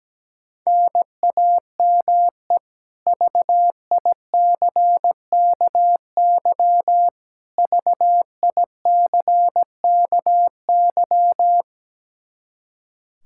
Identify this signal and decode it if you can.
Morse code, heard as a keyed tone, about 17 wpm, 700 Hz. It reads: NAME VICKY VICKY
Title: K1USN SST QSO Practice 17wpm